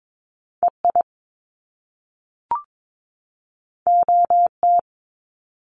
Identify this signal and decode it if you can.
Morse code, heard as a keyed tone, about 22 wpm, 700 Hz. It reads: EI OT
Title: Top Two Letter Combinations in Words 22wpm